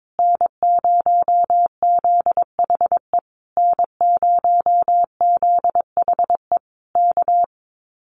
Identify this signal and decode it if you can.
Morse code, heard as a keyed tone, about 22 wpm, 700 Hz. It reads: N0ZHE N0ZHE K
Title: Common QSO Phrases Encore 22wpm